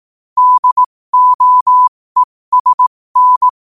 Morse code, heard as a keyed tone, about 18 wpm, spaced as for 15 wpm, 1000 Hz. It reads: DOESN
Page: New quotes every day in morse code at 15 Words per minute.